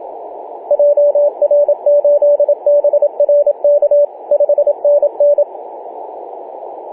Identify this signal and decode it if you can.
Morse code, heard as a keyed tone, about 27 wpm, 580 Hz, with radio noise. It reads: JR8BRK 5NN
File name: posts / oct-08-2033z-tx5z-21-008mhz-cw